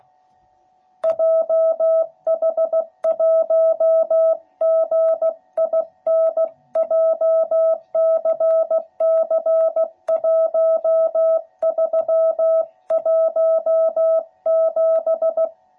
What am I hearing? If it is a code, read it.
Morse code, heard as a keyed tone, about 16 wpm, 660 Hz. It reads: JH1GIN JCC1317
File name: jh8cbh / 2021cw / cw12b